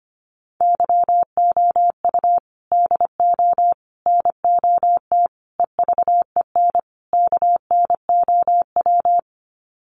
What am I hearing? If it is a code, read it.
Morse code, heard as a keyed tone, about 25 wpm, 700 Hz. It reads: YOU DO NOT EVEN KNOW
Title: Sets of 5 Words Encore2 25wpm